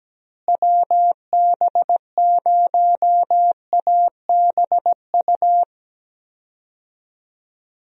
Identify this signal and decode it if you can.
Morse code, heard as a keyed tone, about 17 wpm, 700 Hz. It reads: WB0ABU
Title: USA Call Signs - 6 Characters - Repeated 3x - 17wpm